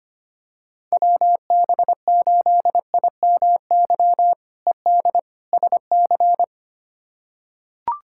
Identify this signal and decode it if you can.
Morse code, heard as a keyed tone, about 25 wpm, 700 Hz. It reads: WB8IMY ED SC
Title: K1USN SST QSO Practice - 25wpm